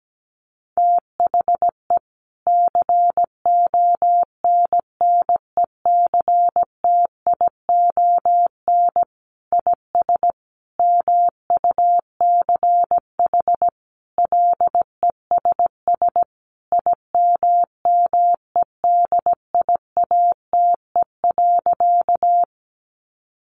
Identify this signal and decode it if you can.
Morse code, heard as a keyed tone, about 17 wpm, 700 Hz. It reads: THE CONNECTION IS MUCH LESS IMMEDIATE.
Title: Sentences from Top 1300 Words 17wpm